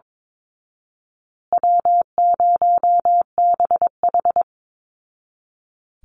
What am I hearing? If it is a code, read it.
Morse code, heard as a keyed tone, about 22 wpm, 700 Hz. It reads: W0BH
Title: Easy Call Signs Repeated 2x - 22wpm